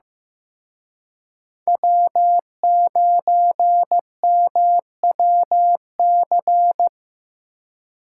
Morse code, heard as a keyed tone, about 18 wpm, 700 Hz. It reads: W9MWC
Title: Speed Racing - USA 1x3 Call Signs 15wpm